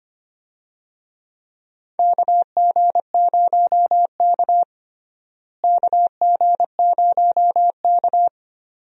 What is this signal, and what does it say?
Morse code, heard as a keyed tone, about 25 wpm, 700 Hz. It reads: KG0K KG0K
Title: USA Call Signs - 4 Characters - Repeated 2x - 25wpm